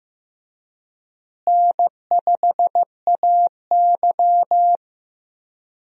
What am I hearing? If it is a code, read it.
Morse code, heard as a keyed tone, about 16 wpm, 700 Hz. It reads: N5AY
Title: USA Call Signs - 4 Characters - Repeated 2x - 15wpm